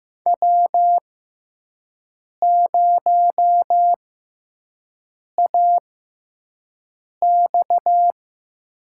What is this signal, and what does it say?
Morse code, heard as a keyed tone, about 15 wpm, spaced as for 5 wpm, 700 Hz. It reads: W0AX
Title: USA Call Signs - 5 Characters - 6x Character Spacing - 15wpm